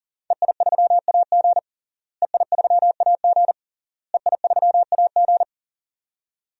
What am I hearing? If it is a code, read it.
Morse code, heard as a keyed tone, about 40 wpm, 700 Hz. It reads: EI3AG EI3AG EI3AG
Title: DXCC-100 Call Signs - 5 Characters - Repeated 3x - 40wpm